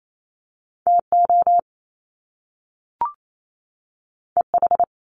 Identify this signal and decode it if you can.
Morse code, heard as a keyed tone, about 28 wpm, 700 Hz. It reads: T O E H
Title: Top Two Letter Combinations in Words 28wpm